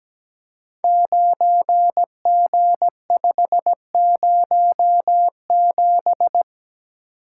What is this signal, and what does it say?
Morse code, heard as a keyed tone, about 17 wpm, 700 Hz. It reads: 9G507
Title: Code Groups - 5 Letter:Numbers - 17wpm